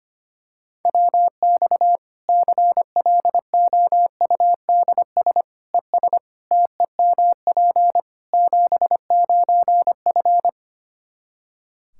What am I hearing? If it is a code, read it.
Morse code, heard as a keyed tone, about 25 wpm, 700 Hz. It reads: WX CLOUDS ES TEMP 79F
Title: Practice QSO - Exchange 2 - Info - 25wpm